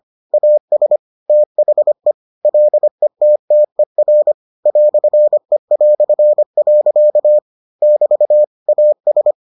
Morse code, heard as a keyed tone, about 25 wpm, 600 Hz. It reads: AS THE LETTER "E". <BT> AS